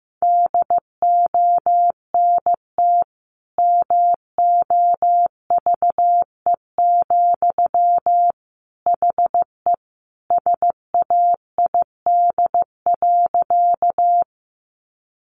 Morse code, heard as a keyed tone, about 15 wpm, 700 Hz. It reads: DONT MOVE, HE SAID.
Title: Sentences from Top 300 Words 15wpm